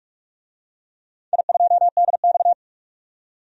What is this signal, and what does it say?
Morse code, heard as a keyed tone, about 46 wpm, 700 Hz. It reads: I2DX
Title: DXCC-100 Call Signs - 4 Characters - 45wpm